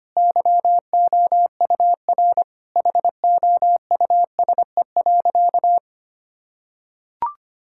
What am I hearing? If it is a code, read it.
Morse code, heard as a keyed tone, about 25 wpm, 700 Hz. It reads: YOUR HOUSE.
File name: practice / 25wpm / Sentences with 2-Word Phrases - 25wpm